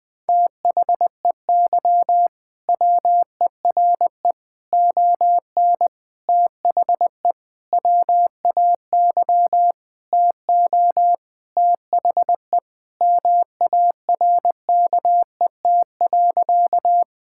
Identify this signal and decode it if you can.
Morse code, heard as a keyed tone, about 20 wpm, 700 Hz. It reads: THEY WERE ON THE WAY TO THE MARKET.
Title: Sentences with 5-Word Phrases - 20wpm